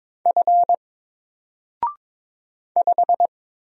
Morse code, heard as a keyed tone, about 22 wpm, 700 Hz. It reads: F 5
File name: Single-Letter-Number-22wpm